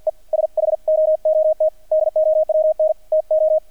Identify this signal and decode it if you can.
Morse code, heard as a keyed tone, about 48 wpm, 630 Hz, with radio noise. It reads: E SHOOT DOWN TO
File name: archived / CRSnet